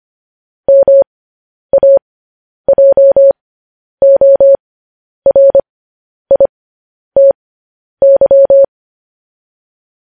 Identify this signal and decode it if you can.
Morse code, heard as a keyed tone, about 25 wpm, spaced as for 10 wpm, 550 Hz. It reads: MAJORITY
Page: news-headlines-in-morse-code-at-10-wpm